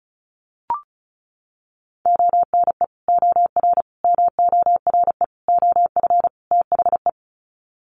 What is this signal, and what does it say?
Morse code, heard as a keyed tone, about 34 wpm, 700 Hz. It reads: ONE OR MORE OF THE
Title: Sets of 5 Words Encore 35wpm